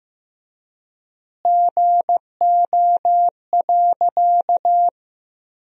Morse code, heard as a keyed tone, about 15 wpm, 700 Hz. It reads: GO.
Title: 4x Word-Spacing - Sentences from Top 400 Words - 15wpm